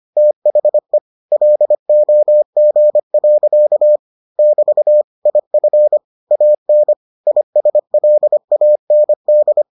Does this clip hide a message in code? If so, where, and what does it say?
Yes: Morse code from 0.2 to 9.6 s, heard as a keyed tone, about 25 wpm, 600 Hz. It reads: THE LOG. <BT> IF AN ISLAND